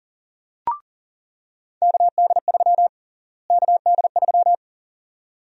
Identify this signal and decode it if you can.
Morse code, heard as a keyed tone, about 40 wpm, 700 Hz. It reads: KD3 KD3
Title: USA Call Sign Prefixes - Repeated 2x - 40wpm